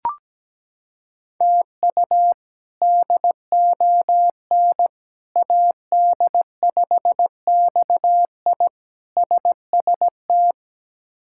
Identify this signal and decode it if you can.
Morse code, heard as a keyed tone, about 17 wpm, 700 Hz. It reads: TU DON AD5XI SST
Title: K1USN SST QSO Practice - 17wpm